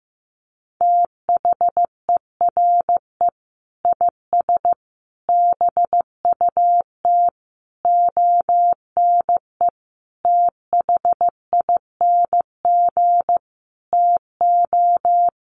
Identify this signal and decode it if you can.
Morse code, heard as a keyed tone, about 15 wpm, 700 Hz. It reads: THERE IS BUT ONE THING TO
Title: Sentences from Top 100 Words 15wpm